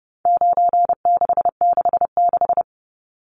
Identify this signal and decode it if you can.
Morse code, heard as a keyed tone, about 30 wpm, 700 Hz. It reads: 9666
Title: Numbers - 4-Digit 30wpm